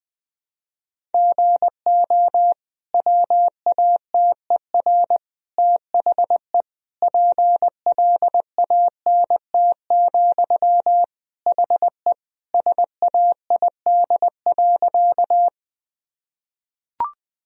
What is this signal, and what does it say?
Morse code, heard as a keyed tone, about 20 wpm, 700 Hz. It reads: GO WATER THE PLANT, HE SAID.
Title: Sentences from Top 1000 Words - Review Part 1 - 20wpm